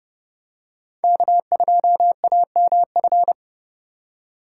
Morse code, heard as a keyed tone, about 28 wpm, 700 Hz. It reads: K2AMF